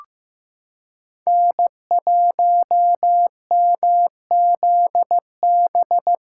Morse code, heard as a keyed tone, about 15 wpm, 700 Hz. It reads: N1MZB
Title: USA Callsign DE Callsign - 5 Characters - 15wpm